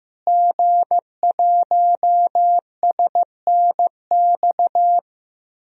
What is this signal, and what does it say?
Morse code, heard as a keyed tone, about 15 wpm, 700 Hz. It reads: G1SNX
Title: DXCC-100 Call Signs - 5 Characters - Repeated 3x - 15wpm